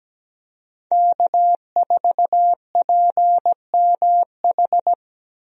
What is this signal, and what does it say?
Morse code, heard as a keyed tone, about 17 wpm, 700 Hz. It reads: K4PMH
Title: USA Call Signs - 5 Characters - Repeated 3x - 17wpm